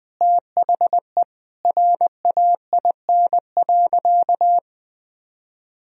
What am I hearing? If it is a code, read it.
Morse code, heard as a keyed tone, about 20 wpm, 700 Hz. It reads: THE RAIN.
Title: Sentences from Top 1000 Words - Review Part 1 - 20wpm